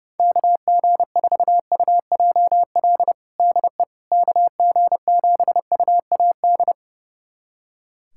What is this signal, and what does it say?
Morse code, heard as a keyed tone, about 30 wpm, 700 Hz. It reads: KG4UJL DE KG7UAD
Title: USA Callsign DE Callsign 30wpm